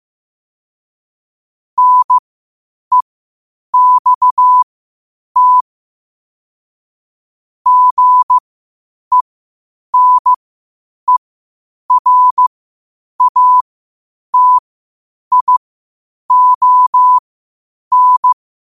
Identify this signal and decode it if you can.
Morse code, heard as a keyed tone, about 15 wpm, spaced as for 8 wpm, 1000 Hz. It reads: NEXT GENERATION
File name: qotdfiles / 8wpm